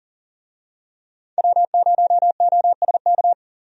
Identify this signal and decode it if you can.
Morse code, heard as a keyed tone, about 40 wpm, 700 Hz. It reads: W0OSK
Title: USA Call Signs - 5 Characters - 40wpm